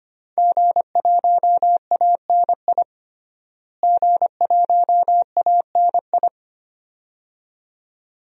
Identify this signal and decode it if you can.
Morse code, heard as a keyed tone, about 25 wpm, 700 Hz. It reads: G1ANI G1ANI
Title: DXCC-100 Call Signs - 5 Characters - Repeated 2x - 25wpm